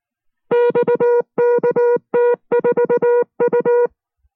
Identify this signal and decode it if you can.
Morse code, heard as a keyed tone, about 19 wpm, 480 Hz. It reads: XKT4U